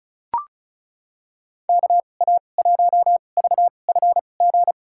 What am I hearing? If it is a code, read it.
Morse code, heard as a keyed tone, about 34 wpm, spaced as for 25 wpm, 700 Hz. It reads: KA1VFG
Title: USA Call Signs - 2x Character Spacing - 35wpm